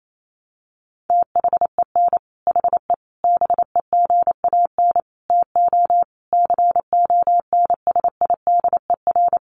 Morse code, heard as a keyed tone, about 28 wpm, 700 Hz. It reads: THEN HE BEGAN TO CONSIDER
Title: Sentences from Top 1200 Words 28wpm